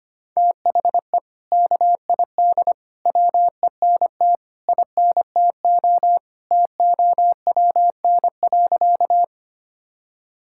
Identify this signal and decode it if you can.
Morse code, heard as a keyed tone, about 25 wpm, 700 Hz. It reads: THE KID WENT INTO TOWN.
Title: Sentences from Top 1600 Words 25wpm